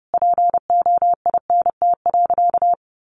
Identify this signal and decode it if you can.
Morse code, heard as a keyed tone, about 30 wpm, 700 Hz. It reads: POINT.
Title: Sentences from Top 1500 Words 30wpm